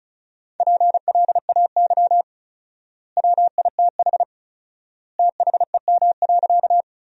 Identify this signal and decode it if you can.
Morse code, heard as a keyed tone, about 34 wpm, 700 Hz. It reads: PLAY WITH THEM.
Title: 4x Word-Spacing - Sentences from Top 200 Words - 35wpm